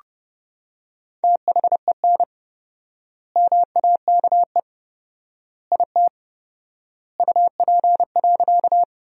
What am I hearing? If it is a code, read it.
Morse code, heard as a keyed tone, about 30 wpm, 700 Hz. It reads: THEN MAKE IT UP.
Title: 4x Word-Spacing - Sentences from Top 100 Words - 30wpm